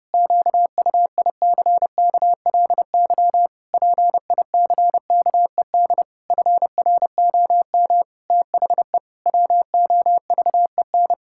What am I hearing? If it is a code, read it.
Morse code, heard as a keyed tone, about 30 wpm, 700 Hz. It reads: QUICKLY PICKED FROM THE WOVEN
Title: Pangram Challenge v2 - 30wpm